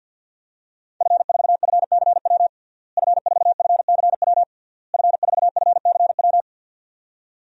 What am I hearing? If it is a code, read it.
Morse code, heard as a keyed tone, about 50 wpm, 700 Hz. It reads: F4FCW F4FCW F4FCW
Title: DXCC-100 Call Signs - 5 Characters - Repeated 3x - 50wpm